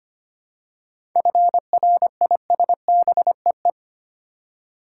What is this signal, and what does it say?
Morse code, heard as a keyed tone, about 25 wpm, 690 Hz. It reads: FRISBEE
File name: practice / 25wpm / Easy Words with Hints - 25wpm